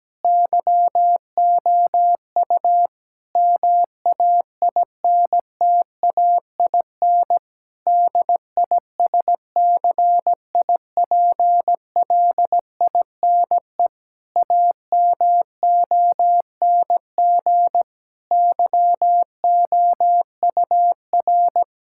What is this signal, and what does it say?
Morse code, heard as a keyed tone, about 17 wpm, 700 Hz. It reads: YOU MAINTAIN DISCIPLINE AMONG YOUR
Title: Sentences from Top 1000 Words - Review Part 6 - 17wpm